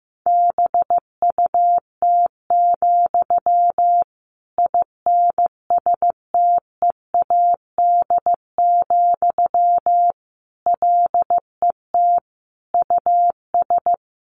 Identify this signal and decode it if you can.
Morse code, heard as a keyed tone, about 15 wpm, 700 Hz. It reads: BUT, INSTEAD, LET US
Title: Sentences from Top 1500 Words 15wpm